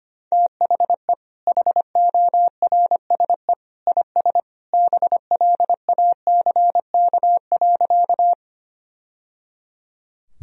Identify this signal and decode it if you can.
Morse code, heard as a keyed tone, about 25 wpm, 700 Hz. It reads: THE HORSE IS BLACK.
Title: Sentences from Top 1000 Words - Review Part 2 - 25wpm